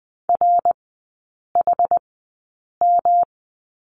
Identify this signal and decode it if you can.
Morse code, heard as a keyed tone, about 20 wpm, spaced as for 8 wpm, 700 Hz. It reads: RHM
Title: 107 Lesson 18e - Letter m - Three Characters 20wpm